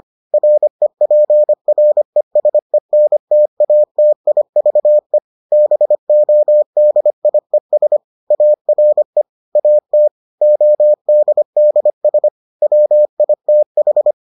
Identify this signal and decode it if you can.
Morse code, heard as a keyed tone, about 25 wpm, 600 Hz. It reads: REPRESENTATIVE BODIES ARE AT ODDS WITH